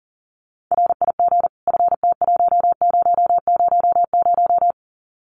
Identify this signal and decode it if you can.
Morse code, heard as a keyed tone, about 39 wpm, 700 Hz. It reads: RIG FT1000
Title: Common QSO Phrases 40wpm